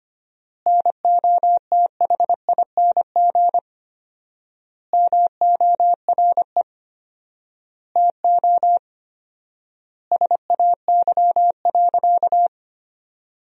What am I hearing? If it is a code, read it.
Morse code, heard as a keyed tone, about 25 wpm, 700 Hz. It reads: NOTHING MORE TO SAY.
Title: 4x Word-Spacing - Sentences from Top 500 Words - 25wpm